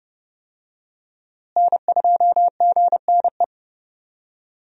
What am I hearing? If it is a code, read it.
Morse code, heard as a keyed tone, about 32 wpm, 700 Hz. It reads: N2GNE
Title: USA Call Signs - 5 Characters - Repeated 3x - 30wpm